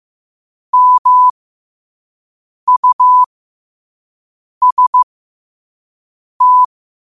Morse code, heard as a keyed tone, about 15 wpm, spaced as for 5 wpm, 1000 Hz. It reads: MUST
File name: qotdfiles / 5wpm